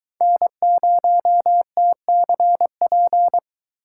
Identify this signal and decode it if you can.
Morse code, heard as a keyed tone, about 21 wpm, 700 Hz. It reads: N0TCP